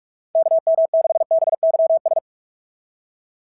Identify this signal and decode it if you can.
Morse code, heard as a keyed tone, about 44 wpm, 660 Hz. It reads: KM6BYS
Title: SOTA QSO Practice 45wpm